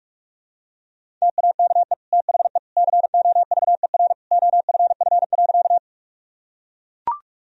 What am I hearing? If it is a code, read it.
Morse code, heard as a keyed tone, about 42 wpm, 700 Hz. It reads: TAKE THE COVER OFF.
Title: Sentences from Top 1000 Words - Review Part 1 - 45wpm